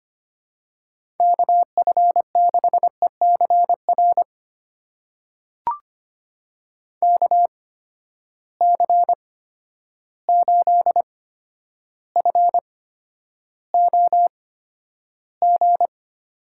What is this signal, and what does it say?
Morse code, heard as a keyed tone, about 25 wpm, spaced as for 11 wpm, 700 Hz. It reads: KF6ECR KC8FOG
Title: USA Call Signs - 6 Characters - 8x Character Spacing - 25wpm